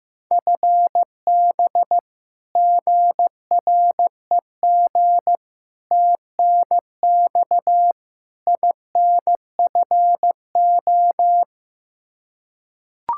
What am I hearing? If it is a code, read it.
Morse code, heard as a keyed tone, about 15 wpm, 700 Hz. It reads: FB GREG TNX INFO
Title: Practice QSO - Exchange 3 - More Info - 15wpm